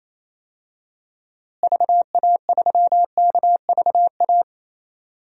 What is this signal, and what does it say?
Morse code, heard as a keyed tone, about 28 wpm, 700 Hz. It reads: VA3KVA
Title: DXCC-100 Call Signs - 6 Characters - Repeated 2x - 28wpm